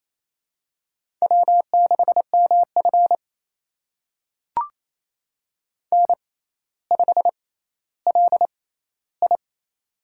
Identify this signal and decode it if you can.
Morse code, heard as a keyed tone, about 28 wpm, spaced as for 14 wpm, 700 Hz. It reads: W6MF N5LI
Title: USA Call Signs - 4 Characters - 6x Character Spacing - 28wpm